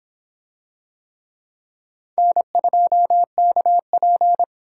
Morse code, heard as a keyed tone, about 25 wpm, 700 Hz. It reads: N2KP